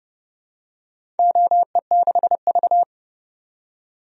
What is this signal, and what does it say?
Morse code, heard as a keyed tone, about 30 wpm, 700 Hz. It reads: OE6V
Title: DXCC-100 Call Signs - 4 Characters - 30wpm